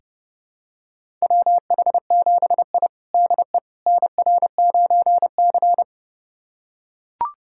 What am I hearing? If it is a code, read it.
Morse code, heard as a keyed tone, about 29 wpm, 700 Hz. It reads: WH7I DE NR9C
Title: USA Callsign DE Callsign - 4 Characters - 30wpm